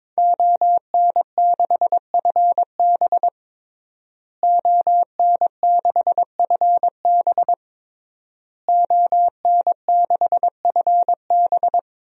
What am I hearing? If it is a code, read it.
Morse code, heard as a keyed tone, about 22 wpm, 700 Hz. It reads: ON6FB ON6FB ON6FB
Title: DXCC-100 Call Signs - 5 Characters - Repeated 3x - 22wpm